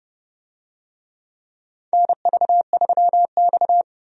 Morse code, heard as a keyed tone, about 30 wpm, 700 Hz. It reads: NV3X
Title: USA Call Signs - 4 Characters - 30wpm